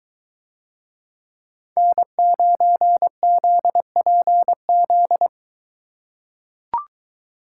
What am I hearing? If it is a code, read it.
Morse code, heard as a keyed tone, about 23 wpm, 700 Hz. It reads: N9ZPZ